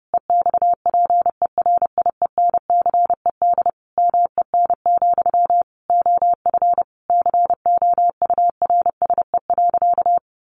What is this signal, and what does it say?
Morse code, heard as a keyed tone, about 30 wpm, 700 Hz. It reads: EXPERIENCED MEN, OF COURSE.
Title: Sentences from Top 1100 Words 30wpm